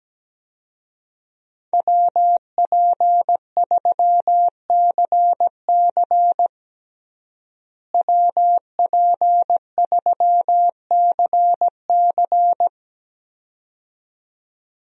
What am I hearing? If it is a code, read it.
Morse code, heard as a keyed tone, about 17 wpm, 700 Hz. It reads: WP3CC WP3CC
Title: USA Call Signs - 5 Characters - Repeated 2x - 17wpm